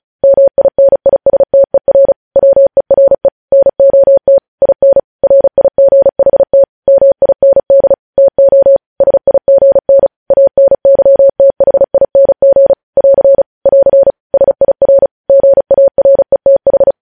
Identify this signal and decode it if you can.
Morse code, heard as a keyed tone, about 36 wpm, 550 Hz. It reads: MINISTER WERE NOT IN RIGHT MIND TO SIGN ANYTHING <AR> <AR> SIR GARETH
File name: podcast / News headlines-1x-35x35